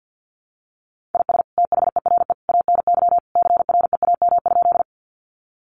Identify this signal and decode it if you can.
Morse code, heard as a keyed tone, about 50 wpm, 700 Hz. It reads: IS THERE ANY CREAM?
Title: Sentences from Top 1500 Words 50wpm